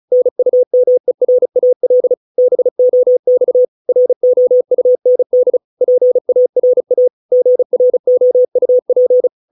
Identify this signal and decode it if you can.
Morse code, heard as a keyed tone, about 34 wpm, 500 Hz. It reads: NUMERAL BOX ROUND PARA GROUP